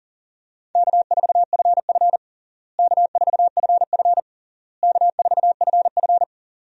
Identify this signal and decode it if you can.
Morse code, heard as a keyed tone, about 37 wpm, 700 Hz. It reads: K4FF K4FF K4FF
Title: USA Call Signs - 4 Characters - Repeated 3x - 40wpm